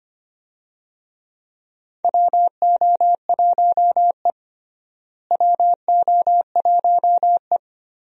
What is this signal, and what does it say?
Morse code, heard as a keyed tone, about 25 wpm, 700 Hz. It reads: WO1E WO1E
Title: USA Call Signs - 4 Characters - Repeated 2x - 25wpm